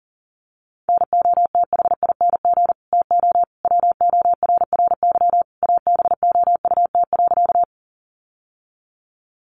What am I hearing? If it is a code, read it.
Morse code, heard as a keyed tone, about 40 wpm, 700 Hz. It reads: NOTHING TO WORRY ABOUT.
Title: Sentences from Top 1100 Words 40wpm